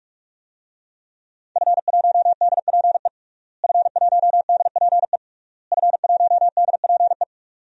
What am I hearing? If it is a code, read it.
Morse code, heard as a keyed tone, about 41 wpm, 700 Hz. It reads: F1DPE F1DPE F1DPE
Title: DXCC-100 Call Signs - 5 Characters - Repeated 3x - 45wpm